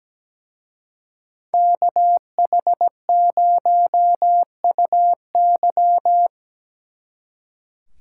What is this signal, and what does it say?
Morse code, heard as a keyed tone, about 18 wpm, 700 Hz. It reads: KH0UY
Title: Speed Racing - USA 2x2 Call Signs 17wpm